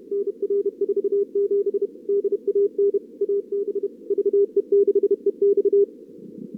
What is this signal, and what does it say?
Morse code, heard as a keyed tone, about 35 wpm, 440 Hz. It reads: NR47 DAN AB VE6EX
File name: CWSprint / Feb2022 / k5tr-feb-na-sprint-cw-k5ot -0100